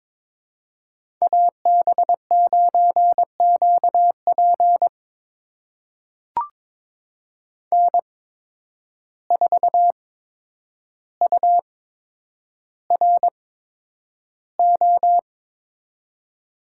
Morse code, heard as a keyed tone, about 22 wpm, spaced as for 9 wpm, 700 Hz. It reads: AB9QP N4URO
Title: USA Call Signs - 5 Characters - 8x Character Spacing - 22wpm